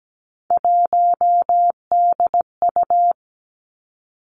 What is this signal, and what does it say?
Morse code, heard as a keyed tone, about 17 wpm, 700 Hz. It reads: 1DU
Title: Three Challenging Letter-Number Combinations 17wpm